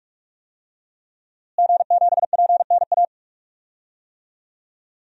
Morse code, heard as a keyed tone, about 46 wpm, 700 Hz. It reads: G7PNA
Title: DXCC-100 Call Signs - 5 Characters - Repeated 3x - 45wpm